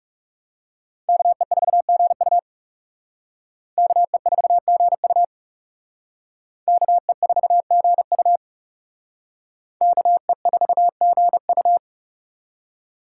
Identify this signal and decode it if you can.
Morse code, heard as a keyed tone, about 36 wpm, 690 Hz. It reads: KE4GU KE4GU KE4GU KE4GU
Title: Speed Racing - USA Call Signs 30wpm